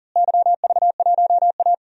Morse code, heard as a keyed tone, about 40 wpm, 700 Hz. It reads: YV1A
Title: DXCC-100 Call Signs - 4 Characters - Repeated 3x - 40wpm